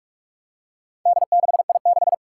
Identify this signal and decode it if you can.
Morse code, heard as a keyed tone, about 46 wpm, 700 Hz. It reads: D6I6
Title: Four Challenging Letter-Number Combinations 45wpm